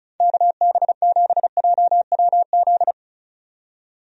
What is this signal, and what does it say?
Morse code, heard as a keyed tone, about 34 wpm, 700 Hz. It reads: KB7JWZ
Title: USA Call Signs - 6 Characters - 35wpm